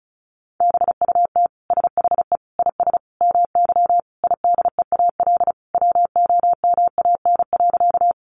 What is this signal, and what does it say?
Morse code, heard as a keyed tone, about 34 wpm, 700 Hz. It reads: BUT SHE IS MY IDEAL WOMAN.
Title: Sentences from Top 1400 Words 35wpm